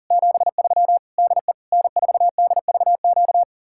Morse code, heard as a keyed tone, about 48 wpm, 700 Hz. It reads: 73 DE N4DVQ
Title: Speed-Racing Common QSO Phrases - 40wpm